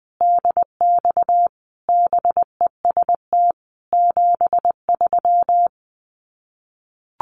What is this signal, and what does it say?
Morse code, heard as a keyed tone, about 20 wpm, 700 Hz. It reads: DX BEST 73
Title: Common QSO Phrases 20wpm